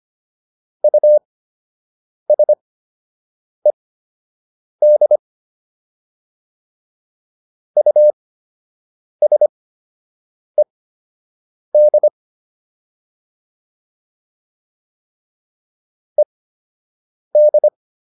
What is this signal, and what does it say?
Morse code, heard as a keyed tone, about 25 wpm, spaced as for 7 wpm, 600 Hz. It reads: USED USED ED